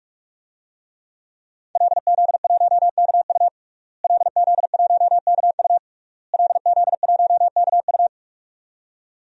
Morse code, heard as a keyed tone, about 46 wpm, 700 Hz. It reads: LZ1KU LZ1KU LZ1KU
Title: DXCC-100 Call Signs - 5 Characters - Repeated 3x - 45wpm